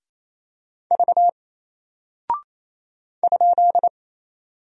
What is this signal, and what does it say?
Morse code, heard as a keyed tone, about 28 wpm, 700 Hz. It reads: V ?
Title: Single Character (Letter:Number:Punctuation:Prosign) Alt 28wpm